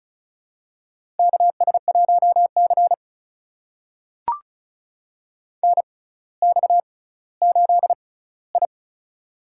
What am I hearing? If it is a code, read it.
Morse code, heard as a keyed tone, about 36 wpm, spaced as for 18 wpm, 690 Hz. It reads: KS1C NX8I
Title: USA Call Signs - 4 Characters - 6x Character Spacing - 35wpm